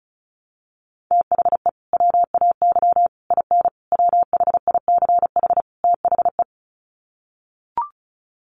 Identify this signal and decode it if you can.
Morse code, heard as a keyed tone, about 34 wpm, 700 Hz. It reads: THE WAY IN WHICH THE
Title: Sets of 5 Words Encore2 35wpm